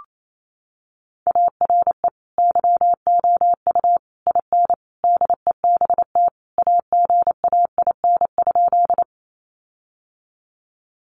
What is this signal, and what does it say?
Morse code, heard as a keyed tone, about 28 wpm, 700 Hz. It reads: ARE YOU IN DEBT AGAIN?
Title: Sentences from Top 1300 Words 28wpm